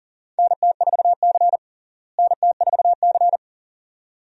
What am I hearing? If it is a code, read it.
Morse code, heard as a keyed tone, about 40 wpm, 700 Hz. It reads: NT4C NT4C
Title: USA Call Signs - 4 Characters - Repeated 2x - 40wpm